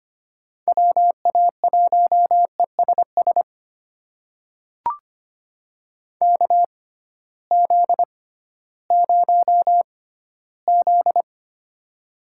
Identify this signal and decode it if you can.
Morse code, heard as a keyed tone, about 25 wpm, spaced as for 15 wpm, 700 Hz. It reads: WA1ESS KZ0Z
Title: USA Call Signs - 6x Character Spacing - 25wpm